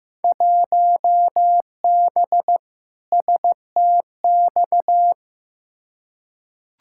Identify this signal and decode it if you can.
Morse code, heard as a keyed tone, about 15 wpm, 700 Hz. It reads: 1B STX
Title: Field Day Exchanges - 15wpm